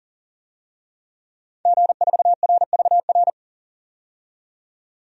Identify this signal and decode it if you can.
Morse code, heard as a keyed tone, about 40 wpm, 700 Hz. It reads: G4RVR
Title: DXCC-100 Call Signs - 5 Characters - 40wpm